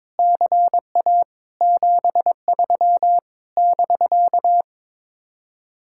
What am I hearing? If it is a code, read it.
Morse code, heard as a keyed tone, about 22 wpm, 700 Hz. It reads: CA 73 <BK>
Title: Speed-Racing Common QSO Phrases - 22wpm